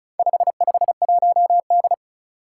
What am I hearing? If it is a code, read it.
Morse code, heard as a keyed tone, about 36 wpm, 700 Hz. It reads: 551D
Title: Four Challenging Letter-Number Combinations 35wpm